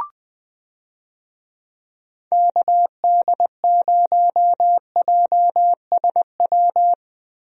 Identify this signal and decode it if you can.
Morse code, heard as a keyed tone, about 21 wpm, 700 Hz. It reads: KD0JSW
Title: USA Call Signs - 6 Characters - Repeated 3x - 20wpm